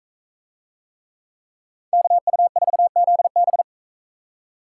Morse code, heard as a keyed tone, about 41 wpm, 700 Hz. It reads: KU4ZB